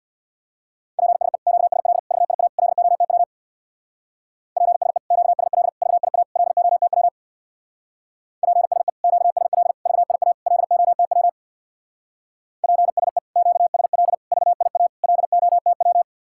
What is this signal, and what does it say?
Morse code, heard as a keyed tone, about 62 wpm, 700 Hz. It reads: PSE QSL VIA LOTW PSE QSL VIA LOTW PSE QSL VIA LOTW PSE QSL VIA LOTW
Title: Speed-Racing Common QSO Phrases - 50wpm